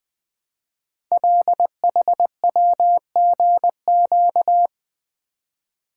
Morse code, heard as a keyed tone, about 20 wpm, 700 Hz. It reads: LHWGQ
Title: Code Groups - 5 Letters - 20wpm